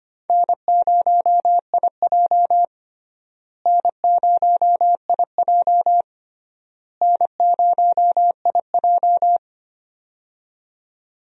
Morse code, heard as a keyed tone, about 25 wpm, 700 Hz. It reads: N0IJ N0IJ N0IJ
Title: USA Call Signs - 4 Characters - Repeated 3x - 25wpm